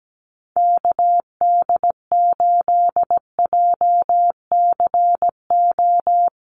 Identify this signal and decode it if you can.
Morse code, heard as a keyed tone, about 17 wpm, 700 Hz. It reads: KD8JCO
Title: USA Call Signs Repeated 3x 17wpm